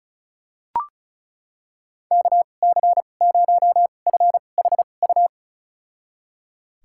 Morse code, heard as a keyed tone, about 34 wpm, spaced as for 25 wpm, 700 Hz. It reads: KC0FHU
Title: USA Call Signs - 6 Characters - 2x Character Spacing - 35wpm